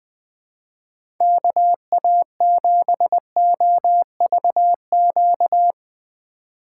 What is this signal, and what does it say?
Morse code, heard as a keyed tone, about 20 wpm, 700 Hz. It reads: KA7OVQ
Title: USA Call Signs - 6 Characters - 2x Character Spacing - 20wpm